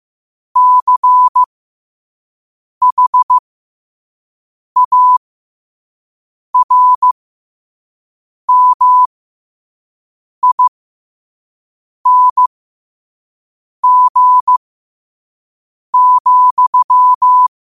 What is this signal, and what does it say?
Morse code, heard as a keyed tone, about 15 wpm, spaced as for 5 wpm, 1000 Hz. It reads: CHARMING,
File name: qotdfiles / 5wpm